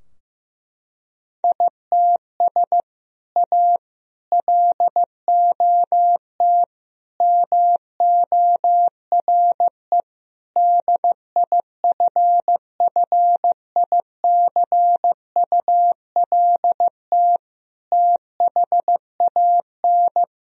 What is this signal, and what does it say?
Morse code, heard as a keyed tone, about 15 wpm, 700 Hz. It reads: ITS A LOT MORE DIFFICULT THAN
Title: Sentences from Top 1000 Words - Review Part 3 - 15wpm